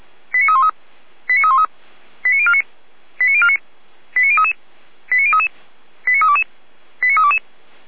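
Five-tone selective call (ZVEI), heard as quick runs of five tones, tones of about 70 ms. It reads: 89313 89313 89049 89049 89033 89033 89322 89322
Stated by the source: ZVEI five-tone call received by SDR, from the feuerwehr-orlamuende.org fire brigade server.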